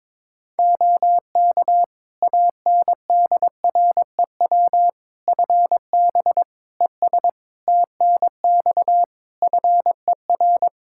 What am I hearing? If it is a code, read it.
Morse code, heard as a keyed tone, about 22 wpm, 700 Hz. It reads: OK ANDREW FB ES TNX FER
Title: Practice QSO - Exchange 2 - Info - 22wpm